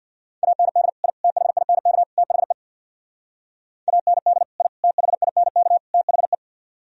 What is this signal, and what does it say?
Morse code, heard as a keyed tone, about 59 wpm, 700 Hz. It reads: AND I THINK THE AND I THINK THE
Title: Speed-Racing - Sets of 4 Words 50wpm